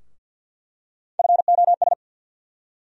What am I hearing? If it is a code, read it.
Morse code, heard as a keyed tone, about 50 wpm, 700 Hz. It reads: FOS
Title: USA Call Sign Postfixes - 3 Letters 50wpm